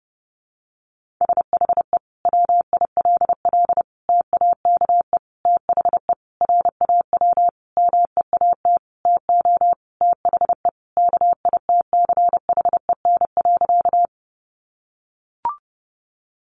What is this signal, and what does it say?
Morse code, heard as a keyed tone, about 30 wpm, 700 Hz. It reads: SHE WILL TAKE THE RAW MEAT TO THE KITCHEN.
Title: Sentences from Top 1500 Words 30wpm